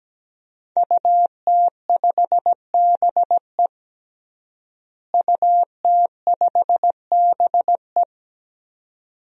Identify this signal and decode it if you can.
Morse code, heard as a keyed tone, about 17 wpm, 700 Hz. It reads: UT5BE UT5BE
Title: DXCC-100 Call Signs - 5 Characters - Repeated 2x - 17wpm